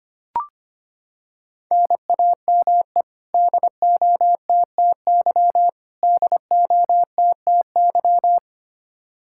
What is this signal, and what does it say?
Morse code, heard as a keyed tone, about 25 wpm, 700 Hz. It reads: NAME DOTTY DOTTY
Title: Practice QSO - Exchange 1 - Beginning - 25wpm